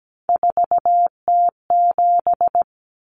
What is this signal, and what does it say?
Morse code, heard as a keyed tone, about 17 wpm, 700 Hz. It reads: 4T7
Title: Common Cut Numbers - 3-Digit 17wpm